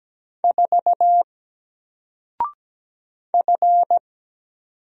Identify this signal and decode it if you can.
Morse code, heard as a keyed tone, about 17 wpm, 700 Hz. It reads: 4 F
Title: Single Letter-Number Alt 17wpm